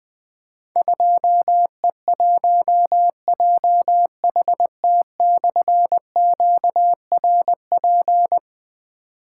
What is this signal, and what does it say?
Morse code, heard as a keyed tone, about 20 wpm, 700 Hz. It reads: 2E1JHT/QRP
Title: Difficult Call Signs 20wpm